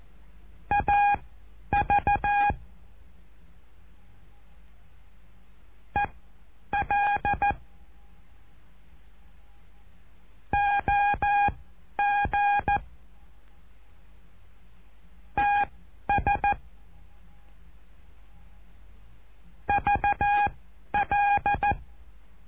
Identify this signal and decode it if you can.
Morse code, heard as a keyed tone, about 14 wpm, spaced as for 10 wpm, 820 Hz. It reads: AV EL OG TS VL
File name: cursos / CURSO DE TELEGRAFIA / Telegrafia - Trilha 08